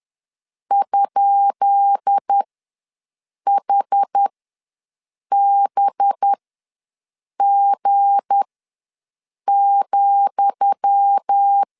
Morse code, heard as a keyed tone, about 10 wpm, spaced as for 6 wpm, 800 Hz. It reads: ?HBG,
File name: LEZIONI / 10 SIGNS / 10D COMA ws 10wpm 4314